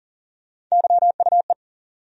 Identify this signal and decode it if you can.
Morse code, heard as a keyed tone, about 40 wpm, 700 Hz. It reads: YUE
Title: USA Call Sign Postfixes - 3 Letters 40wpm